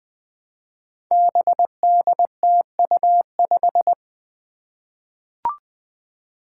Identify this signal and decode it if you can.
Morse code, heard as a keyed tone, about 20 wpm, 700 Hz. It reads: BDTU5
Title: Five Challenging Letter-Number Combinations 20wpm